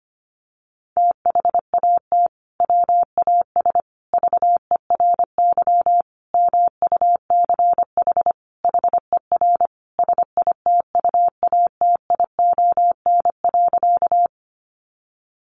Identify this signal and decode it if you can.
Morse code, heard as a keyed tone, about 25 wpm, 700 Hz. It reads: THAT WAS VERY MUCH HER SITUATION.
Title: Sentences from Top 1200 Words 25wpm